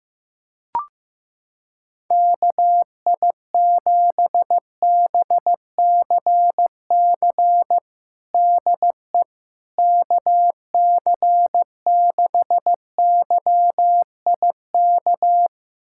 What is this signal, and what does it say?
Morse code, heard as a keyed tone, about 15 wpm, 700 Hz. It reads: KI7BCC DE KC6YIK
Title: USA Callsign DE Callsign 15wpm